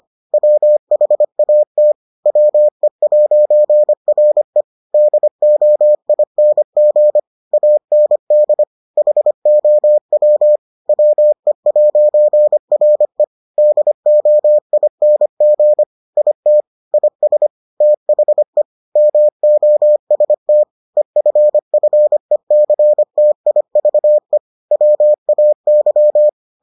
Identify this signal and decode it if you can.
Morse code, heard as a keyed tone, about 25 wpm, 600 Hz. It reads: WHAT WE'RE DOING AND HOW WE'RE DOING IT IS THE MOST EFFECTIVE WAY